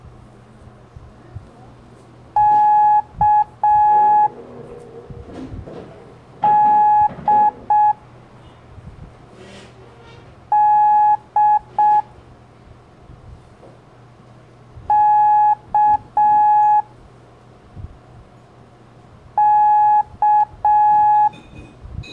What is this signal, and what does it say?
Morse code, heard as a keyed tone, about 6 wpm, 840 Hz. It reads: K D D K K
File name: blog / LETRAS COM SOM SEMELHANTE MORSE